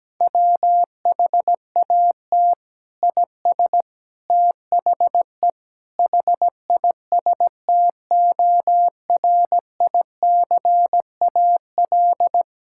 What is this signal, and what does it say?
Morse code, heard as a keyed tone, about 17 wpm, 700 Hz. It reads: WHAT IS THE HISTORICAL
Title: Sentences from Top 1000 Words - Review Part 5 - 17wpm